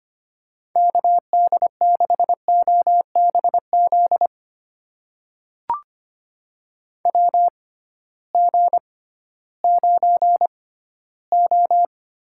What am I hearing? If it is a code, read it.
Morse code, heard as a keyed tone, about 25 wpm, spaced as for 15 wpm, 700 Hz. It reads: KD6OBZ WG9O
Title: USA Call Signs - 6x Character Spacing - 25wpm